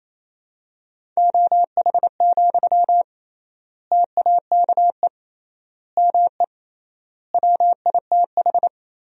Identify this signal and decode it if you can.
Morse code, heard as a keyed tone, about 28 wpm, 700 Hz. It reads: OH, TAKE ME WITH
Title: 3x Word-Spacing - Sentences from Top 500 Words - 28wpm